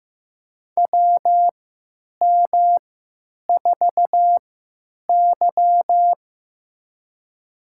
Morse code, heard as a keyed tone, about 15 wpm, spaced as for 9 wpm, 700 Hz. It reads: WM4Y
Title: USA Call Signs - 4 Characters - 3x Character Spacing - 15wpm